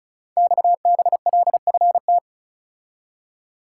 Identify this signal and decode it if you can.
Morse code, heard as a keyed tone, about 36 wpm, 700 Hz. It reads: XBLFT
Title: Code Groups - 5 Letters - 35wpm